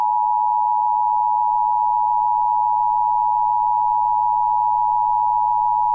Emergency Alert System (SAME) tones (the 853 and 960 Hz attention tone). It is heard from 1.0 to 5.4 s.